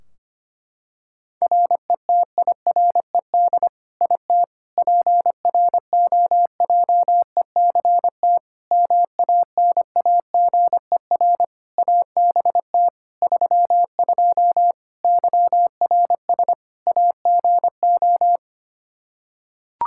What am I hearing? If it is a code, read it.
Morse code, heard as a keyed tone, about 25 wpm, 700 Hz. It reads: RETIRED IT PROJECT MANAGER ABT 32 YRS AGO
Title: Practice QSO - Exchange 3 - More Info - 25wpm